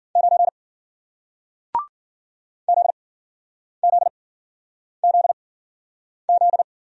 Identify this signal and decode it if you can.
Morse code, heard as a keyed tone, about 51 wpm, spaced as for 9 wpm, 700 Hz. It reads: 9 ZZZZ